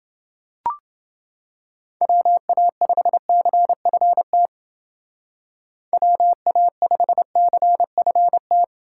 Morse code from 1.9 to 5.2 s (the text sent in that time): WA5CFT